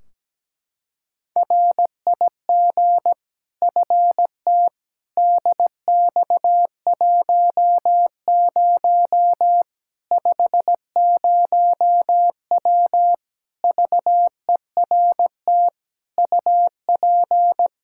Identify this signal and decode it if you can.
Morse code, heard as a keyed tone, about 17 wpm, 700 Hz. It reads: RIG FT DX10 50W VERT UP
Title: Practice QSO - Exchange 2 - Info - 17wpm